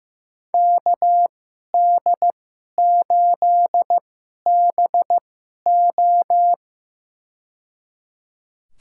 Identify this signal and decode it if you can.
Morse code, heard as a keyed tone, about 15 wpm, spaced as for 11 wpm, 700 Hz. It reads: KD8BO
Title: USA Call Signs - 2x Character Spacing - 15wpm